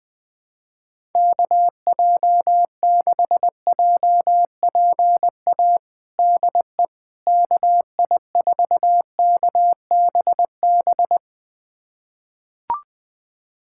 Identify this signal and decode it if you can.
Morse code, heard as a keyed tone, about 20 wpm, 700 Hz. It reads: KJ6JPA DE KI4KBB
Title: USA Callsign DE Callsign - 6 Characters - 20wpm